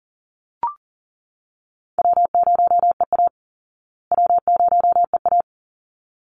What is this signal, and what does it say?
Morse code, heard as a keyed tone, about 40 wpm, 700 Hz. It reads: W0EA W0EA
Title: Easy Call Signs Repeated 2x - 40wpm